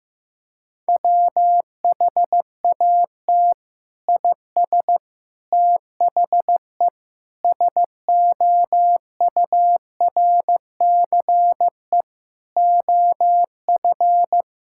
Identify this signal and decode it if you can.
Morse code, heard as a keyed tone, about 15 wpm, 700 Hz. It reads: WHAT IS THE SOURCE OF
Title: Sentences from Top 1000 Words - Review Part 5 - 15wpm